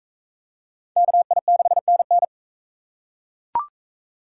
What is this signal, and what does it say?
Morse code, heard as a keyed tone, about 40 wpm, 700 Hz. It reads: KI6NN